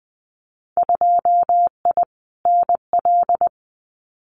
Morse code, heard as a keyed tone, about 20 wpm, 700 Hz. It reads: 2I NL
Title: Winter Field Day Exchanges Encore 20wpm